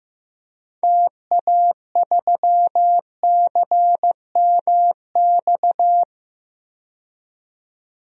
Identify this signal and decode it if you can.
Morse code, heard as a keyed tone, about 15 wpm, 700 Hz. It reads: TA3CMX
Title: DXCC-100 Call Signs - 6 Characters - 15wpm